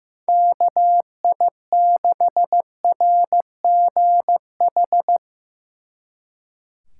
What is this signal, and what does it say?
Morse code, heard as a keyed tone, about 16 wpm, 700 Hz. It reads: KI6RGH
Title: Speed Racing - USA 2x3 Call Signs 15wpm